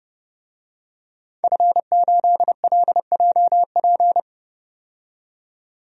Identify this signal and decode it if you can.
Morse code, heard as a keyed tone, about 32 wpm, 700 Hz. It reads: F8LJP
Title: DXCC-100 Call Signs - 5 Characters - 30wpm